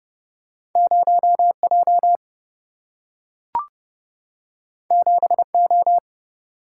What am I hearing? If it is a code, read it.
Morse code, heard as a keyed tone, about 30 wpm, 700 Hz. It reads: 0 J 7 O
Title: Single-Letter Single-Number 30wpm